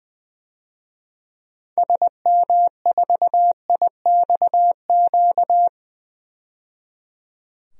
Morse code, heard as a keyed tone, about 21 wpm, 700 Hz. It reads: SM4IXQ
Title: DXCC-100 Call Signs - 6 Characters - 20wpm